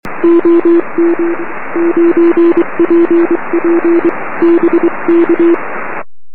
Morse code, heard as a keyed tone, about 23 wpm, 340 Hz. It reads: OM 9PP BK
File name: om3pv 70mhz / zaznam / oz9pp